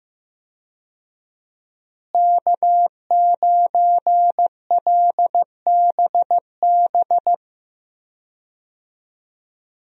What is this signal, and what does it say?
Morse code, heard as a keyed tone, about 15 wpm, 700 Hz. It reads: K9LBB
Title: USA Call Signs - 5 Characters - Repeated 2x - 15wpm